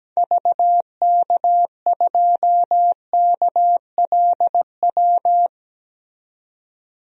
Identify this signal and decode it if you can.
Morse code, heard as a keyed tone, about 17 wpm, 700 Hz. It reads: VK2KLW
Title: DXCC-100 Call Signs - 6 Characters - 17wpm